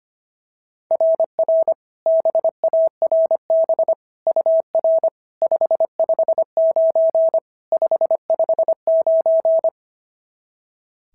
Morse code, heard as a keyed tone, about 25 wpm, 690 Hz. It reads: RR BARB UR 559 559
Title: SOTA QSO Practice 25wpm